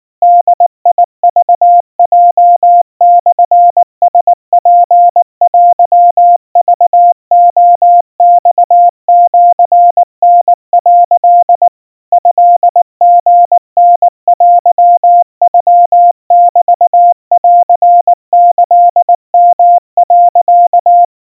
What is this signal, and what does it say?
Morse code (Japanese Wabun code), heard as a keyed tone, about 19 wpm, 700 Hz. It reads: ボクヲモラツテクレマシタ」 トリタテノメンキヨ、
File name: kawa / Morse sample / kamyu 70 cpm 700 Hz Crispy